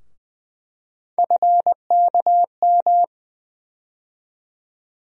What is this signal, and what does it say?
Morse code, heard as a keyed tone, about 20 wpm, 700 Hz. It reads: FKM
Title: USA Call Sign Postfixes - 3 Letters 20wpm